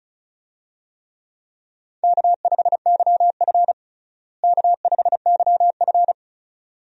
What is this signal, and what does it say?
Morse code, heard as a keyed tone, about 36 wpm, 700 Hz. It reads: K5YF K5YF
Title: USA Call Signs - 4 Characters - Repeated 2x - 35wpm